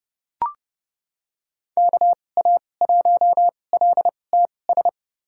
Morse code, heard as a keyed tone, about 30 wpm, spaced as for 22 wpm, 700 Hz. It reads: KA1LTS
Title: USA Call Signs - 6 Characters - 2x Character Spacing - 30wpm